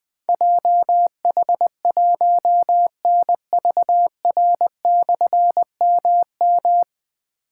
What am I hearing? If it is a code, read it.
Morse code, heard as a keyed tone, about 20 wpm, 700 Hz. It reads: JH1NVR/MM
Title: Difficult Call Signs 20wpm